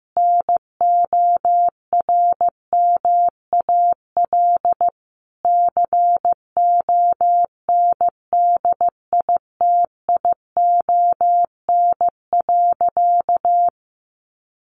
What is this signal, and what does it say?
Morse code, heard as a keyed tone, about 14 wpm, 700 Hz. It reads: NORMAL CONDITION.
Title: Sentences from Top 1200 Words 15wpm